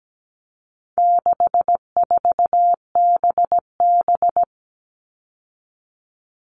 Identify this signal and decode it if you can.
Morse code, heard as a keyed tone, about 17 wpm, 700 Hz. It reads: 64BB
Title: All Cut Numbers - 4-Digit 17wpm